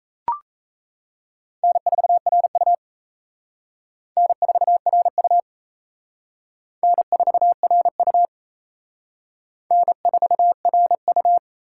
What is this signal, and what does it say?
Morse code, heard as a keyed tone, about 34 wpm, 700 Hz. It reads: N4RU N4RU N4RU N4RU
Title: Speed Racing - USA 1x2 Call Signs 28wpm